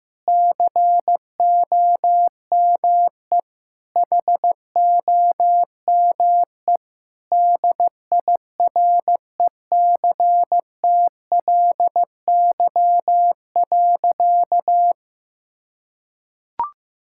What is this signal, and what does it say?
Morse code, heard as a keyed tone, about 15 wpm, 700 Hz. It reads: COME HOME DIRECTLY.
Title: Sentences from Top 1000 Words - Review Part 5 - 15wpm